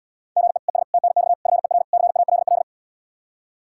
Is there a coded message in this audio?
Yes: Morse code from 0.4 to 2.6 s, heard as a keyed tone, about 78 wpm, 700 Hz. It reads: GE ES TNX FER CALL